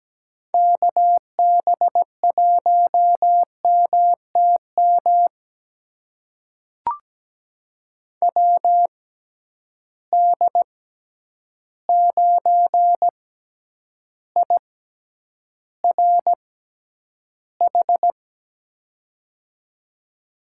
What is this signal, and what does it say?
Morse code, heard as a keyed tone, about 17 wpm, spaced as for 9 wpm, 700 Hz. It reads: KB1MTM WD9IRH
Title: USA Call Signs - 6 Characters - 6x Character Spacing - 17wpm